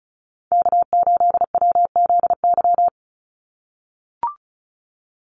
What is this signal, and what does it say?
Morse code, heard as a keyed tone, about 34 wpm, 700 Hz. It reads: K8WZY
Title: USA Call Signs 35wpm